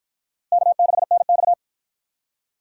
Morse code, heard as a keyed tone, about 59 wpm, 700 Hz. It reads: K6NX